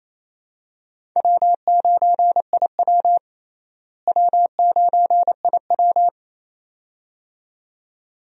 Morse code, heard as a keyed tone, about 28 wpm, 700 Hz. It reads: W9IW W9IW
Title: USA Call Signs - 4 Characters - Repeated 2x - 28wpm